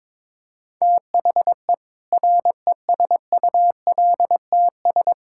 Morse code, heard as a keyed tone, about 22 wpm, 700 Hz. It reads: THE RESULTS
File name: Sentences with 7-Word Phrases - 22wpm